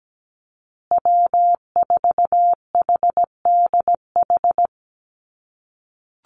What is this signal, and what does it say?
Morse code, heard as a keyed tone, about 17 wpm, 700 Hz. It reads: W4HDH
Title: USA Call Signs Repeated 2x 17wpm